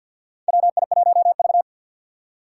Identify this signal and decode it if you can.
Morse code, heard as a keyed tone, about 48 wpm, 690 Hz. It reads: WI1V
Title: USA Call Signs - 4 Characters - 6x Character Spacing - 50wpm